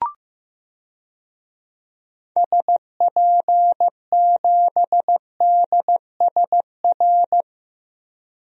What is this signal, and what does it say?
Morse code, heard as a keyed tone, about 16 wpm, 700 Hz. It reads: SP7DSR
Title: DXCC-100 Call Signs - 6 Characters - Repeated 3x - 15wpm